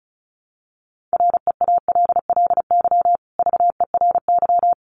Morse code, heard as a keyed tone, about 34 wpm, 700 Hz. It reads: REALLY VERY
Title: Sentences from Top 600 Words 35wpm